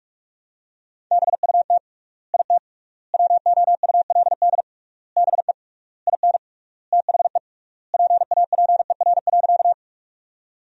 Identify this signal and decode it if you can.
Morse code, heard as a keyed tone, about 47 wpm, 700 Hz. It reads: BUT IT WOULD BE IN THE PAPER.
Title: 3x Word-Spacing - Sentences from Top 300 Words - 45wpm